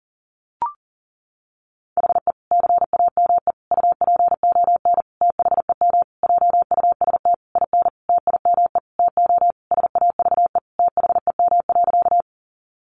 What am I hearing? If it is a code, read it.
Morse code, heard as a keyed tone, about 40 wpm, 700 Hz. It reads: HE CAME UPON THEM JUST IN TIME TO SAVE THEM.
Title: Sentences from Top 1100 Words 40wpm